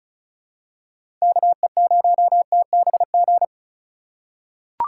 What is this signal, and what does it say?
Morse code, heard as a keyed tone, about 36 wpm, 700 Hz. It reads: KE0TBG
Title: USA Call Signs - 2x Character Spacing - 35wpm